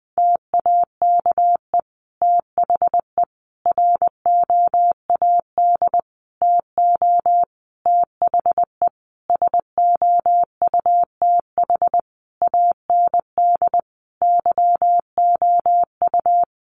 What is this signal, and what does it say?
Morse code, heard as a keyed tone, about 20 wpm, 700 Hz. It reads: TAKE THE ROAD TO THE SOUTH AND YOU
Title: Sentences from Top 400 Words 20wpm